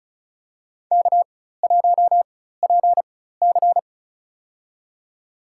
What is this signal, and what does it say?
Morse code, heard as a keyed tone, about 36 wpm, spaced as for 16 wpm, 700 Hz. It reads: K1PC
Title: USA Call Signs - 4x Character Spacing - 35wpm